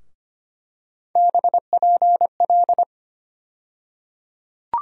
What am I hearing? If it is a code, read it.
Morse code, heard as a keyed tone, about 25 wpm, 700 Hz. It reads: BPL
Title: USA Call Sign Postfixes - 3 Letters 25wpm